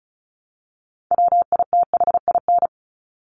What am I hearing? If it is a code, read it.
Morse code, heard as a keyed tone, about 34 wpm, 700 Hz. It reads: WITHIN
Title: Top 501-750 Words 35wpm